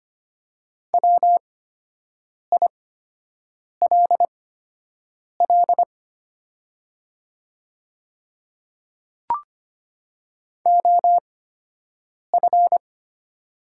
Morse code, heard as a keyed tone, about 25 wpm, spaced as for 7 wpm, 700 Hz. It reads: WILL OF
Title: 8x Character Spacing - Top 200 Words - 25wpm